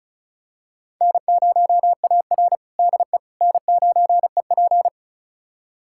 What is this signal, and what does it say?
Morse code, heard as a keyed tone, about 36 wpm, 700 Hz. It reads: N0AR DE N9EP
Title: USA Callsign DE Callsign - 4 Characters - 35wpm